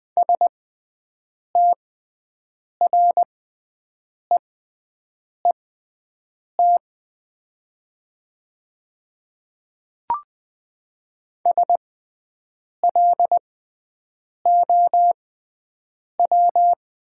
Morse code, heard as a keyed tone, about 20 wpm, spaced as for 7 wpm, 700 Hz. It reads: STREET SLOW
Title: 6x Character Spacing - Top 500 Words - 20wpm